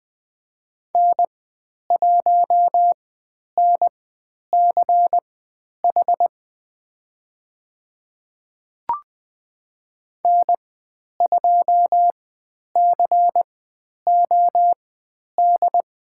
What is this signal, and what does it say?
Morse code, heard as a keyed tone, about 20 wpm, spaced as for 10 wpm, 700 Hz. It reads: N1NCH N2COD
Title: 109 Lesson 18g - Letter m - Call Signs 20wpm